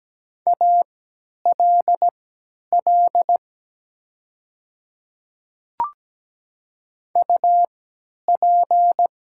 Morse code, heard as a keyed tone, about 17 wpm, spaced as for 10 wpm, 700 Hz. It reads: ALL UP
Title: 3x Character Spacing - Top 100 Words - 17wpm